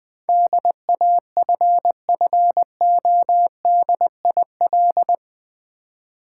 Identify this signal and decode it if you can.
Morse code, heard as a keyed tone, about 20 wpm, 700 Hz. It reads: DAFFODIL
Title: Difficult Words with Hints - 20wpm